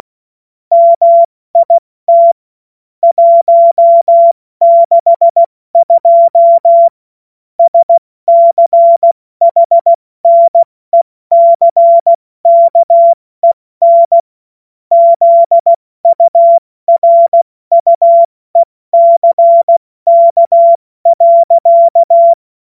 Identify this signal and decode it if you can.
Morse code, heard as a keyed tone, about 16 wpm, spaced as for 14 wpm, 680 Hz. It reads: MIT 162 SCHNECKEN ZURUECK.
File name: Witz-23-September-2025-70-F80